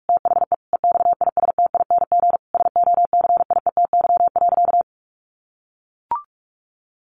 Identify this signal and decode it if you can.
Morse code, heard as a keyed tone, about 44 wpm, 700 Hz. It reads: THE EXISTING SOCIETY.
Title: Sentences from Top 1500 Words 45wpm